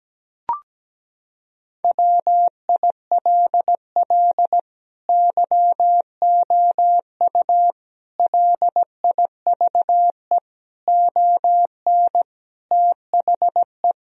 Morse code, heard as a keyed tone, about 17 wpm, 700 Hz. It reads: WILL YOU LIVE ON THE
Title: Sentences from Top 1000 Words - Review Part 3 - 17wpm